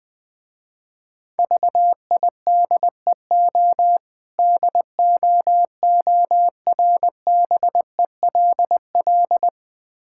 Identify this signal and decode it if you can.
Morse code, heard as a keyed tone, about 20 wpm, 700 Hz. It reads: VIDEO DOORBELL
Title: Difficult Words - Speed-Racing with Hints - 20wpm